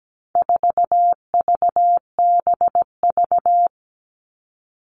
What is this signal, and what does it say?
Morse code, heard as a keyed tone, about 17 wpm, 700 Hz. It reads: 4VBV
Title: All Cut Numbers - 4-Digit 17wpm